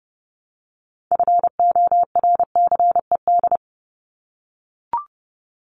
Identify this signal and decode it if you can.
Morse code, heard as a keyed tone, about 29 wpm, 700 Hz. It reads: FORCED
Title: ED Suffix - Top 100 Words with Base Words 30wpm